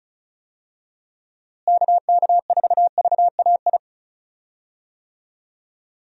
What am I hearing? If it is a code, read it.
Morse code, heard as a keyed tone, about 35 wpm, 700 Hz. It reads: KK4VAI
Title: USA Call Signs - 6 Characters - Repeated 2x - 35wpm